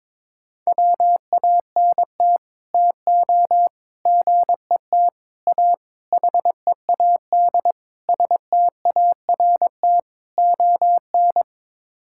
Morse code, heard as a keyed tone, about 22 wpm, 700 Hz. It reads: WANT TO GET A HEAD START ON
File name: practice / 22wpm / Sets of 7 Words - 22wpm